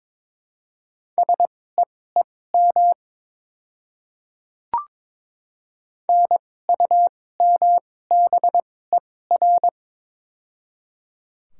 Morse code, heard as a keyed tone, about 22 wpm, spaced as for 16 wpm, 700 Hz. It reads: SEEM NUMBER
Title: 2x Character Spacing - Top 300 Words - 22wpm